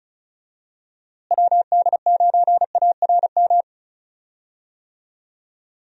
Morse code, heard as a keyed tone, about 36 wpm, 700 Hz. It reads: WD9ARM
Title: USA Call Signs - 6 Characters - Repeated 2x - 35wpm